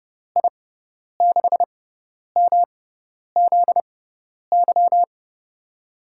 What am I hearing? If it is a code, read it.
Morse code, heard as a keyed tone, about 30 wpm, spaced as for 10 wpm, 700 Hz. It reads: I6MZY
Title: USA Call Signs - 6 Characters - 6x Character Spacing - 30wpm